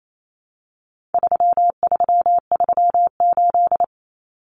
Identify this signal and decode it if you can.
Morse code, heard as a keyed tone, about 28 wpm, 700 Hz. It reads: 3338
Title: Common Cut Numbers - 4-Digit 28wpm